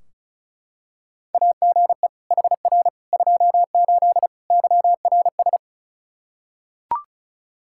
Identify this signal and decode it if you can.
Morse code, heard as a keyed tone, about 34 wpm, 700 Hz. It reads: AGE HR 28 YRS
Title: Practice QSO - Exchange 3 - More Info - 35wpm